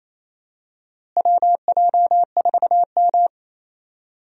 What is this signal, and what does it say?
Morse code, heard as a keyed tone, about 28 wpm, 700 Hz. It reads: WJ4M
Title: USA Call Signs - 4 Characters - Repeated 3x - 28wpm